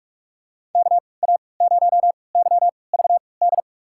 Morse code, heard as a keyed tone, about 44 wpm, spaced as for 25 wpm, 700 Hz. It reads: KA0YVD
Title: USA Call Signs - 6 Characters - 3x Character Spacing - 45wpm